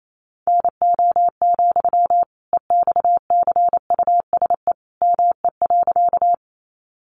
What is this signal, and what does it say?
Morse code, heard as a keyed tone, about 28 wpm, 700 Hz. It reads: NO, EXCUSE ME.
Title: Sentences from Top 700 Words 28wpm